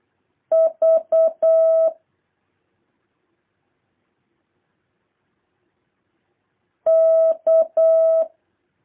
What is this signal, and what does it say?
Morse code, heard as a keyed tone, about 8 wpm, 650 Hz. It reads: V K